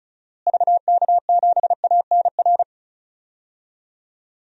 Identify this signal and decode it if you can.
Morse code, heard as a keyed tone, about 36 wpm, 700 Hz. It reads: VK7ANR
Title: DXCC-100 Call Signs - 6 Characters - Repeated 2x - 35wpm